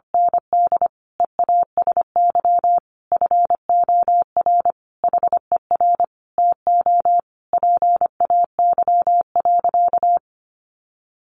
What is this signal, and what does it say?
Morse code, heard as a keyed tone, about 25 wpm, 700 Hz. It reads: ND EASY FOR HER TO PAY.
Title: Sentences from Top 1500 Words 25wpm